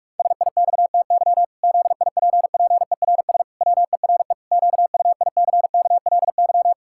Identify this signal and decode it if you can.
Morse code, heard as a keyed tone, about 46 wpm, 700 Hz. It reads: SIXTY ZIPPERS WERE QUICKLY
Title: Pangram Challenge v2 - 45wpm